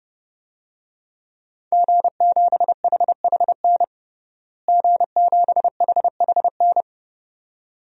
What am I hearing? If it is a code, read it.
Morse code, heard as a keyed tone, about 31 wpm, 700 Hz. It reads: G7HHN G7HHN
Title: DXCC-100 Call Signs - 5 Characters - Repeated 2x - 30wpm